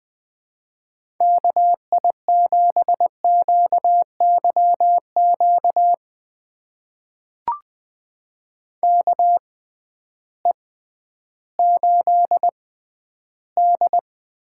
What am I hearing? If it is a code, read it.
Morse code, heard as a keyed tone, about 20 wpm, spaced as for 12 wpm, 700 Hz. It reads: KI7QYQ KE8D
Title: USA Call Signs - 6 Characters - 6x Character Spacing - 20wpm